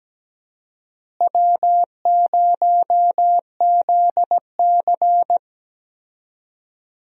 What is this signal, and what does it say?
Morse code, heard as a keyed tone, about 17 wpm, 700 Hz. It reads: W0ZC
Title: USA Call Signs - 4 Characters - Repeated 3x - 17wpm